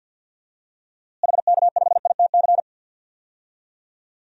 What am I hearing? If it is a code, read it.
Morse code, heard as a keyed tone, about 48 wpm, 700 Hz. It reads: HK5ITC
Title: DXCC-100 Call Signs - 6 Characters - Repeated 2x - 50wpm